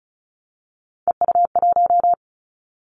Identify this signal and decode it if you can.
Morse code, heard as a keyed tone, about 34 wpm, 700 Hz. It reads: EU1
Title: Callsign Prefixes for Top 100 DXCC - 35wpm